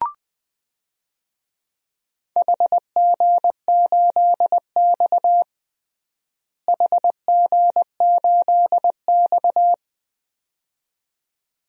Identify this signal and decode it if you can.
Morse code, heard as a keyed tone, about 20 wpm, 700 Hz. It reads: HG8X HG8X
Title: DXCC-100 Call Signs - 4 Characters - Repeated 2x - 20wpm